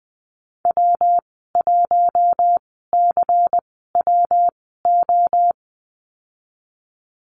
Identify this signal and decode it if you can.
Morse code, heard as a keyed tone, about 20 wpm, spaced as for 14 wpm, 700 Hz. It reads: W1CWO
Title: 161 Lesson 26g - Letter g - Call Signs 20wpm